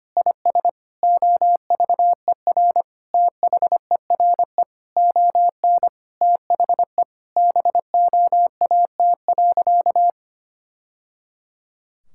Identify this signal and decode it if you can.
Morse code, heard as a keyed tone, about 25 wpm, 700 Hz. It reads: IS OVER THERE ON THE BOAT.
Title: Sentences from Top 1000 Words - Review Part 5 - 25wpm